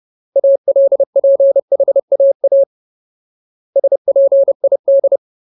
Morse code, heard as a keyed tone, about 30 wpm, 550 Hz. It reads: ALPHAA SPID
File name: Ultimate Word List / 30WPM Ultimate